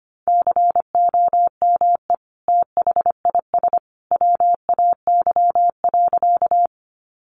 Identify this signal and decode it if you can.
Morse code, heard as a keyed tone, about 25 wpm, 700 Hz. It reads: COME THIS WAY.
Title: Sentences from Top 100 Words 25wpm